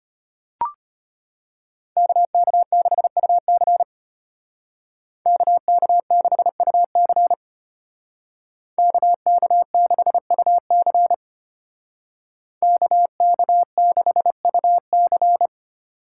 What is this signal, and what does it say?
Morse code, heard as a keyed tone, about 33 wpm, 700 Hz. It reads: KK6UC KK6UC KK6UC KK6UC